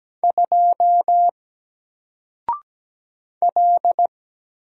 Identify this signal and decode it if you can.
Morse code, heard as a keyed tone, about 17 wpm, 700 Hz. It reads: 2 L
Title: Single Character (Letter:Number:Punctuation:Prosign) Alt 17wpm